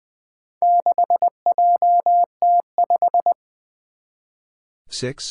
Morse code, heard as a keyed tone, about 20 wpm, 700 Hz. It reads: 6JT5
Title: Four Challenging Letter-Number Combinations 20wpm